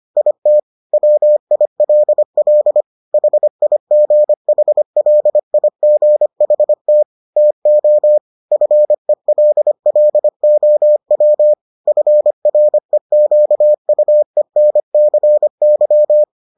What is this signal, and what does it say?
Morse code, heard as a keyed tone, about 25 wpm, 600 Hz. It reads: IT WILL HIGHLIGHT TO FELLOW FREQUENCY